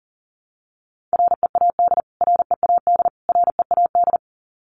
Morse code, heard as a keyed tone, about 40 wpm, 700 Hz. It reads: READ READ READ
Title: Top 200 Words Repeated 40wpm